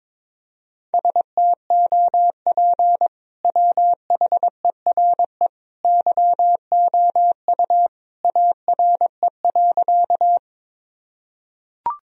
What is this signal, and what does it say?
Morse code, heard as a keyed tone, about 22 wpm, 700 Hz. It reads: STOP WHERE YOU ARE.
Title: Sentences from Top 1000 Words - Review Part 1 - 22wpm